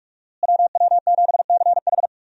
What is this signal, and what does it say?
Morse code, heard as a keyed tone, about 43 wpm, 700 Hz. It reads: WW7CH
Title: USA Call Signs - 5 Characters - Repeated 2x - 45wpm